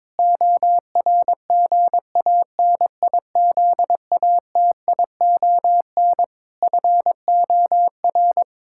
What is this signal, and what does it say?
Morse code, heard as a keyed tone, about 22 wpm, 700 Hz. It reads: ORGANIZATION FOR
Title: Sentences from Top 1000 Words - Review Part 5 - 22wpm